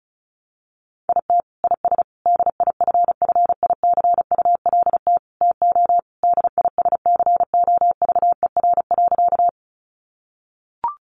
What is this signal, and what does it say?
Morse code, heard as a keyed tone, about 34 wpm, 700 Hz. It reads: IT IS DIFFICULT TO DISCOVER.
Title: Sentences from Top 1100 Words 35wpm